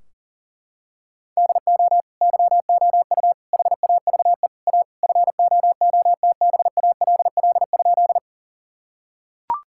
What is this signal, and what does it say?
Morse code, heard as a keyed tone, about 40 wpm, 700 Hz. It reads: DO YOU HAVE A FOOTBALL?
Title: Sentences from Top 1600 Words 40wpm